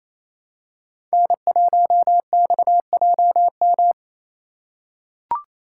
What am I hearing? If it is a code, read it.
Morse code, heard as a keyed tone, about 28 wpm, 700 Hz. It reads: N1XJM
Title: K1USN SST QSO Practice - 28wpm